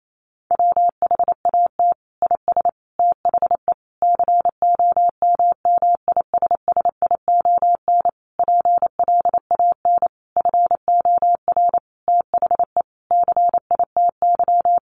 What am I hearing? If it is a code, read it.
Morse code, heard as a keyed tone, about 28 wpm, 700 Hz. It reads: WHAT IS THE COMMISSION PLAN FOR THE CITY
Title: Sentences from Top 1400 Words 28wpm